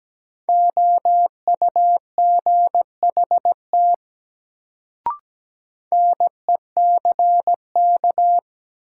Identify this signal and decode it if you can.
Morse code, heard as a keyed tone, about 17 wpm, 700 Hz. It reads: OUGHT NECK
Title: Sets of 5 Words Encore - New Word Drill - 17wpm